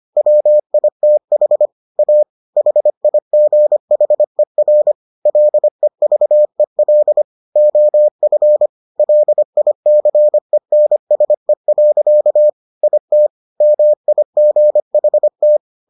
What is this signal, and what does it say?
Morse code, heard as a keyed tone, about 25 wpm, 600 Hz. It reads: WITH A HIGHER LEVEL OF LICENSE. IT MIGHT